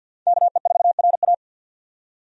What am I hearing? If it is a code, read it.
Morse code, heard as a keyed tone, about 47 wpm, 700 Hz. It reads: KE4RA
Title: USA Call Signs - 5 Characters - 50wpm